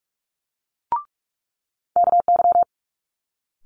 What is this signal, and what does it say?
Morse code, heard as a keyed tone, about 46 wpm, 700 Hz. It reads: KY
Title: Multi-letter Phonemes Quick Fire 45wpm